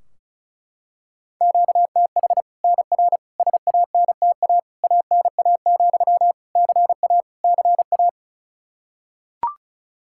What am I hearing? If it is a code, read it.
Morse code, heard as a keyed tone, about 36 wpm, 700 Hz. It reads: QTH NR SANTA ANA, CA CA
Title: Practice QSO - Exchange 1 - Beginning - 35wpm